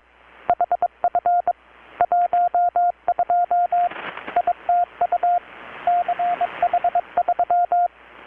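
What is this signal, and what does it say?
Morse code, heard as a keyed tone, about 23 wpm, 720 Hz, with radio noise. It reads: HF 12 ITU CH3
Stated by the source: Morse from KPH's Night of Nights XX maritime radiotelegraphy event, received in Chiapas, Mexico about 3,800 km away on a Kenwood R-600 with a grounded 10 m long-wire antenna.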